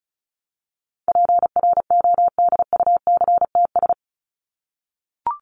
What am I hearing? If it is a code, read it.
Morse code, heard as a keyed tone, about 34 wpm, 700 Hz. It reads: PRODUCTS
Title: ES:S Suffix - Top 100 Words with Base Words 35wpm